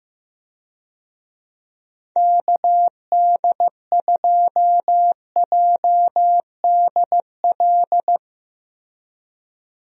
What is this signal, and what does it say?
Morse code, heard as a keyed tone, about 15 wpm, 700 Hz. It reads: KD2JDL
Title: USA Call Signs - 6 Characters - Repeated 3x - 15wpm